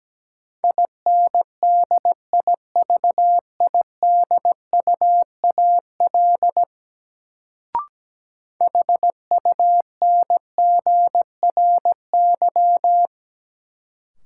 Morse code, heard as a keyed tone, about 17 wpm, 700 Hz. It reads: INDIVIDUAL HUNGRY
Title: Sentences from Top 1400 Words - New Word Drill Alt - 17wpm